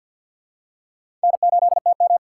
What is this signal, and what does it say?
Morse code, heard as a keyed tone, about 53 wpm, 700 Hz. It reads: N8TM
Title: USA Call Signs - 3x Character Spacing - 50wpm